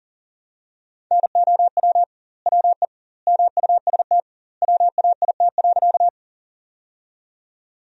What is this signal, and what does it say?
Morse code, heard as a keyed tone, about 41 wpm, 700 Hz. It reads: NOW WE MUST WAIT.
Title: 2x Word-Spacing - Sentences from Top 500 Words - 40wpm